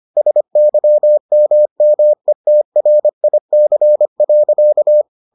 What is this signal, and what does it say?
Morse code, heard as a keyed tone, about 25 wpm, 600 Hz. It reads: SYMMETRIC.